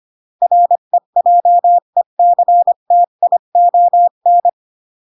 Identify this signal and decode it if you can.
Morse code, heard as a keyed tone, about 25 wpm, 700 Hz. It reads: REJECTION
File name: CW / jodorowsky / 82 commandments 25-20